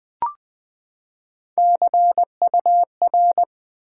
Morse code, heard as a keyed tone, about 20 wpm, 700 Hz. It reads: CUR
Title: Build A Word - 3 Letters At A Time - 9-Letter Words - 20wpm